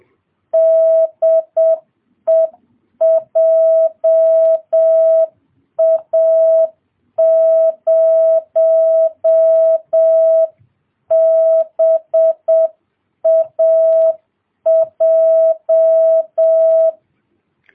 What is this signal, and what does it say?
Morse code, heard as a keyed tone, about 7 wpm, 650 Hz. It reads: DE JA0BAJ